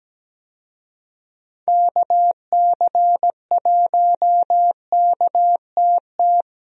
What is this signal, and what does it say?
Morse code, heard as a keyed tone, about 17 wpm, 700 Hz. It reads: KC1KTT
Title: USA Call Signs - 6 Characters - Repeated 3x - 17wpm